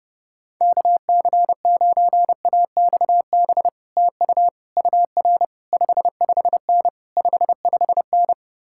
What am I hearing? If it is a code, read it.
Morse code, heard as a keyed tone, about 33 wpm, 700 Hz. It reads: KC9AXB TU UR 55N 55N